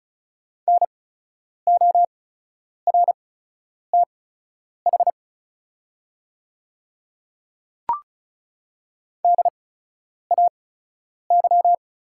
Morse code, heard as a keyed tone, about 36 wpm, spaced as for 10 wpm, 700 Hz. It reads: NORTH DAY
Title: 8x Character Spacing - Top 300 Words - 35wpm